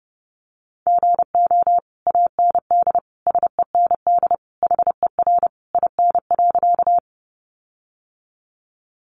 Morse code, heard as a keyed tone, about 30 wpm, 700 Hz. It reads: GO AND SEND HER IN.
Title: Sentences from Top 1500 Words 30wpm